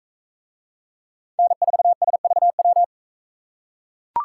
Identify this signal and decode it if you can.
Morse code, heard as a keyed tone, about 43 wpm, 700 Hz. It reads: N4SVW